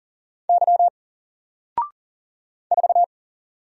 Morse code, heard as a keyed tone, about 40 wpm, 700 Hz. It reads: Y 4
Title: Single Letter-Number 40wpm